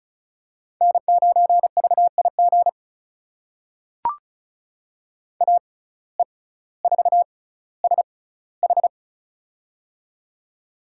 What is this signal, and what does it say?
Morse code, heard as a keyed tone, about 36 wpm, spaced as for 18 wpm, 700 Hz. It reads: N9VIG AE4SH
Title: USA Call Signs - 5 Characters - 6x Character Spacing - 35wpm